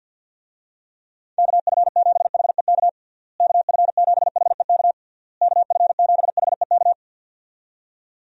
Morse code, heard as a keyed tone, about 52 wpm, 700 Hz. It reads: KF7HEK KF7HEK KF7HEK
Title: USA Call Signs - 6 Characters - Repeated 3x - 50wpm